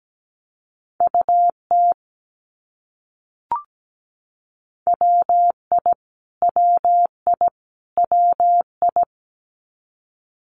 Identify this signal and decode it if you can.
Morse code, heard as a keyed tone, about 17 wpm, 700 Hz. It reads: UT WI WI WI
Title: US States Repeated 3x 17wpm